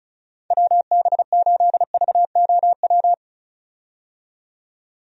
Morse code, heard as a keyed tone, about 34 wpm, 700 Hz. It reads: WB8VOW
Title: USA Call Signs - 6 Characters - 35wpm